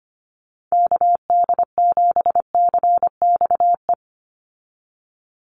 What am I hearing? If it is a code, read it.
Morse code, heard as a keyed tone, about 25 wpm, 700 Hz. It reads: KD7CXE
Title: USA Call Signs 25wpm